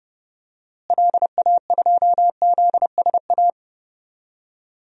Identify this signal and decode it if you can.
Morse code, heard as a keyed tone, about 30 wpm, 700 Hz. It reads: LA2ZSA
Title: DXCC-100 Call Signs - 6 Characters - Repeated 2x - 30wpm